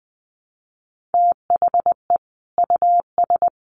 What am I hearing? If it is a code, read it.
Morse code, heard as a keyed tone, about 20 wpm, 700 Hz. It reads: THE US
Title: Sets of 5 Words Encore2 20wpm